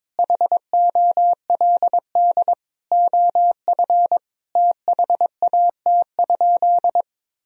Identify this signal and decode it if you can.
Morse code, heard as a keyed tone, about 22 wpm, 700 Hz. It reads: HOLD OF THAT?
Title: Sentences from Top 1000 Words - Review Part 2 - 22wpm